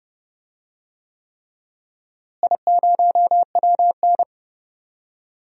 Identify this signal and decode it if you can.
Morse code, heard as a keyed tone, about 30 wpm, 700 Hz. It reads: I0WN
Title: DXCC-100 Call Signs - 4 Characters - Repeated 3x - 30wpm